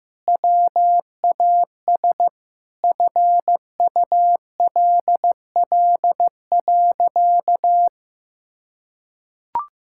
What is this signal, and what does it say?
Morse code, heard as a keyed tone, about 15 wpm, 700 Hz. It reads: WAS FULL.
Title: Sentences from Top 1600 Words 15wpm